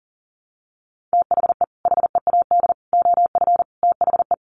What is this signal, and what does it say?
Morse code, heard as a keyed tone, about 40 wpm, 700 Hz. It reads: THE HEAD OF THE
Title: Sets of 4 Words 40wpm